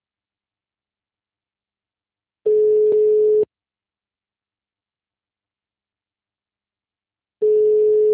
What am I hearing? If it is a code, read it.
Ringback tone (Europe, 425 Hz).
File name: out- -2010- - - .190